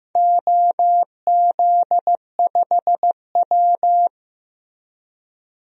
Morse code, heard as a keyed tone, about 15 wpm, 700 Hz. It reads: OZ5W
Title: DXCC-100 Call Signs - 4 Characters - Repeated 2x - 15wpm